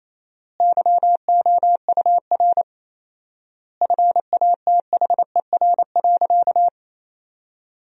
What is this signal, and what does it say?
Morse code, heard as a keyed tone, about 28 wpm, 700 Hz. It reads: YOUR FATHER.
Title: 4x Word-Spacing - Sentences from Top 300 Words - 28wpm